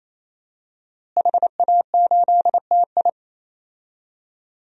Morse code, heard as a keyed tone, about 27 wpm, 700 Hz. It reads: HA8TI
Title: DXCC-100 Call Signs - 5 Characters - Repeated 3x - 28wpm